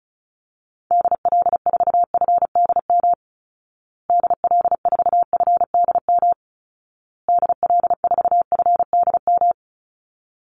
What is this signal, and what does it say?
Morse code, heard as a keyed tone, about 34 wpm, 700 Hz. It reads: DL4FDM DL4FDM DL4FDM
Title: Call Signs Repeated 2x - 35wpm